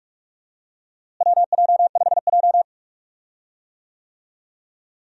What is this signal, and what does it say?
Morse code, heard as a keyed tone, about 42 wpm, 700 Hz. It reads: WJ5J
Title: USA Call Signs - 4 Characters - Repeated 2x - 45wpm